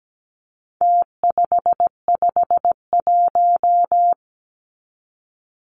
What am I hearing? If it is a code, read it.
Morse code, heard as a keyed tone, about 17 wpm, 700 Hz. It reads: T551
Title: Numbers - Contest Style - 4-Digit 17wpm